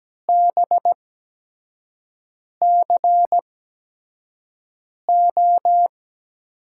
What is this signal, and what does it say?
Morse code, heard as a keyed tone, about 17 wpm, spaced as for 5 wpm, 700 Hz. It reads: BCO
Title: USA Call Signs - 6 Characters - 8x Character Spacing - 17wpm